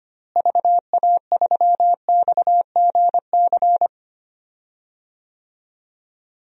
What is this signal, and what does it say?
Morse code, heard as a keyed tone, about 25 wpm, 700 Hz. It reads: VA3XGC
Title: DXCC-100 Call Signs - 6 Characters - Repeated 3x - 25wpm